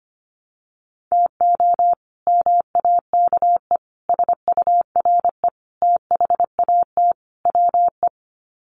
Morse code, heard as a keyed tone, about 25 wpm, 700 Hz. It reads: TO MAKE SURE THAT WE
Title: Sets of 5 Words Encore 25wpm